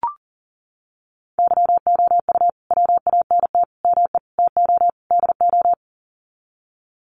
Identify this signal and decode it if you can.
Morse code, heard as a keyed tone, about 40 wpm, 700 Hz. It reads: YOU WANT ME TO DO
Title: Sets of 5 Words Encore2 40wpm